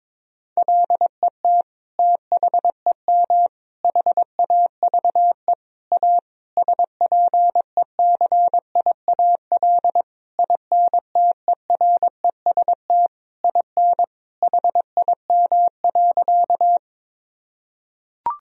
Morse code, heard as a keyed tone, about 22 wpm, 700 Hz. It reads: LET THEM HAVE A SPECIAL INTEREST IN HIM.
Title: Sentences from Top 1000 Words - Review Part 3 - 22wpm